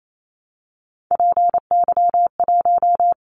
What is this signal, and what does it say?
Morse code, heard as a keyed tone, about 28 wpm, 700 Hz. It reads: PY1
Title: Callsign Prefixes for Top 100 DXCC - 28wpm